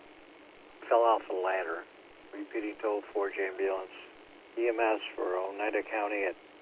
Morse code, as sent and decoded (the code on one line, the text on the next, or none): none